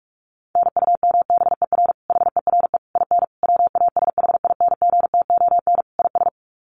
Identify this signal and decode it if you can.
Morse code, heard as a keyed tone, about 44 wpm, 700 Hz. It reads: NUMBER HERE IN WASHINGTON IS
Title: Sets of 5 Words Encore2 45wpm